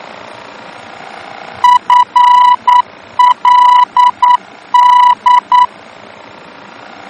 Morse code, heard as a keyed tone, about 9 wpm, 980 Hz. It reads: FLD
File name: germany / mx / FLD-D-VOR